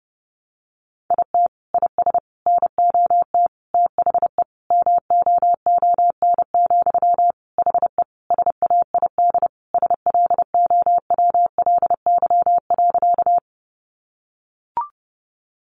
Morse code, heard as a keyed tone, about 30 wpm, 700 Hz. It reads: IT IS NOT THE MOON, HE SAID SLOWLY.
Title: Sentences from Top 1100 Words 30wpm